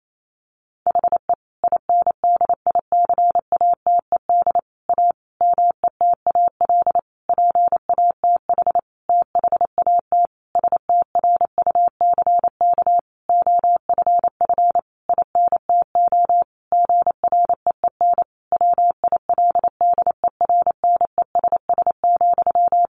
Morse code, heard as a keyed tone, about 28 wpm, 700 Hz. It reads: HE INDICATED A METAL PATH THAT STRUCK OFF INTO GREEN WILDERNESS,
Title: A Sound Of Thunder 28wpm